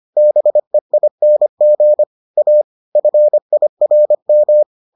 Morse code, heard as a keyed tone, about 25 wpm, 600 Hz. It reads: BEING A FIRM